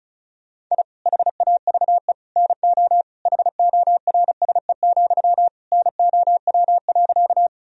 Morse code, heard as a keyed tone, about 36 wpm, 700 Hz. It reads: I HAVE NO HORSE, NOW.
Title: Sentences from Top 1000 Words - Review Part 1 - 35wpm